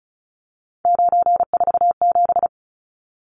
Morse code, heard as a keyed tone, about 34 wpm, 700 Hz. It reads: 947
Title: Numbers - 3-Digit 35wpm